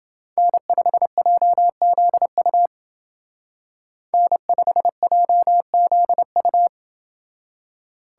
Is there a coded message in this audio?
Yes: Morse code from 0.4 to 6.7 s, heard as a keyed tone, about 29 wpm, 700 Hz. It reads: N5JZU N5JZU